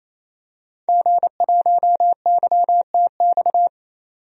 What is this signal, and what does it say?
Morse code, heard as a keyed tone, about 28 wpm, 700 Hz. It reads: G1YTX
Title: DXCC-100 Call Signs - 5 Characters - 28wpm